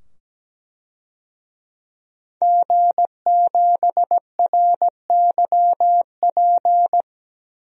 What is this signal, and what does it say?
Morse code, heard as a keyed tone, about 17 wpm, 700 Hz. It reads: G7RYP
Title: DXCC-100 Call Signs - 5 Characters - 17wpm